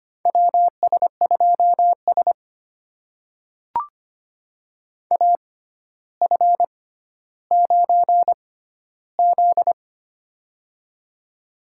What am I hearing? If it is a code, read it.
Morse code, heard as a keyed tone, about 25 wpm, spaced as for 13 wpm, 700 Hz. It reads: WS2S AF9Z
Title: USA Call Signs - 4 Characters - 6x Character Spacing - 25wpm